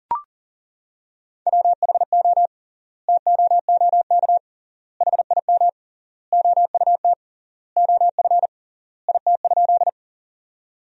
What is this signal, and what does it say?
Morse code, heard as a keyed tone, about 40 wpm, 700 Hz. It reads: WHO TOOK HIM OUT OF IT?
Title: 3x Word-Spacing - Sentences from Top 300 Words - 40wpm